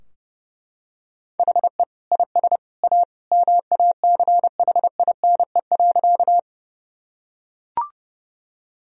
Morse code, heard as a keyed tone, about 30 wpm, 700 Hz. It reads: HE IS A MACHINE.
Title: Sentences from Top 1000 Words - Review Part 2 - 30wpm